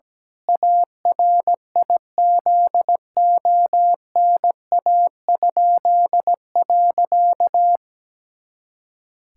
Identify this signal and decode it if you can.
Morse code, heard as a keyed tone, about 17 wpm, 700 Hz. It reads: ARIZONA?.
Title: Pangram Challenge v2 - 17wpm